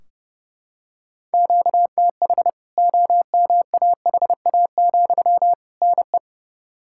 Morse code, heard as a keyed tone, about 30 wpm, 700 Hz. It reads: QTH OMAHA, NE
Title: K1USN SST QSO Practice 30wpm